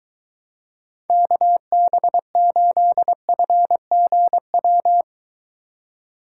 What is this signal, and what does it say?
Morse code, heard as a keyed tone, about 23 wpm, 700 Hz. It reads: KB8FGW